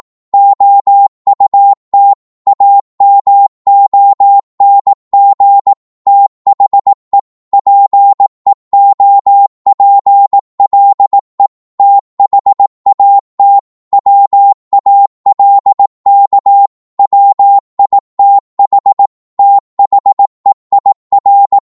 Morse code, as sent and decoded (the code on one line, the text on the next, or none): --- ..- - / .- -- --- -. --. / - .... / .--. --- .--. .-.. / - .... .- - / .-- .- .-.. -.- / .-- .. - .... / - .... .. .-.
OUT AMONG THE PEOPLE THAT WALK WITH THEIR